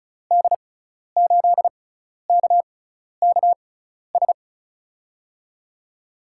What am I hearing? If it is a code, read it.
Morse code, heard as a keyed tone, about 36 wpm, spaced as for 12 wpm, 700 Hz. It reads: D8KKS
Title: USA Call Signs - 6 Characters - 6x Character Spacing - 35wpm